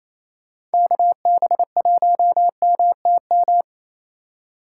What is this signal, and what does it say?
Morse code, heard as a keyed tone, about 28 wpm, 700 Hz. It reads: KB1MTM
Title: USA Call Signs - 6 Characters - 4x Character Spacing - 28wpm